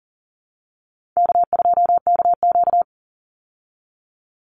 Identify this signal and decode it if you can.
Morse code, heard as a keyed tone, about 40 wpm, 700 Hz. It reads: K2KQ
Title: Call Signs Repeated 3x - 40wpm